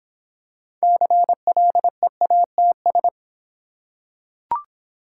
Morse code, heard as a keyed tone, about 26 wpm, 700 Hz. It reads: CLEATS